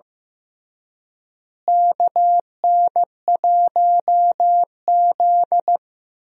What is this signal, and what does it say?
Morse code, heard as a keyed tone, about 15 wpm, 700 Hz. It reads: KN1Z
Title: USA Call Signs - 4 Characters - Repeated 3x - 15wpm